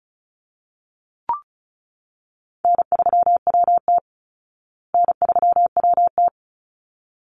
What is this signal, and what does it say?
Morse code, heard as a keyed tone, about 34 wpm, 700 Hz. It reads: N3WT N3WT
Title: Easy Call Signs Repeated 2x - 35wpm